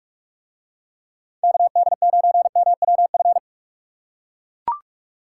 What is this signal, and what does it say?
Morse code, heard as a keyed tone, about 40 wpm, 700 Hz. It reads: KD9MWF